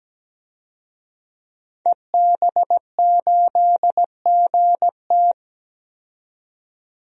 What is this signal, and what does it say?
Morse code, heard as a keyed tone, about 17 wpm, 700 Hz. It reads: EB8GT
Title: DXCC-100 Call Signs - 5 Characters - 17wpm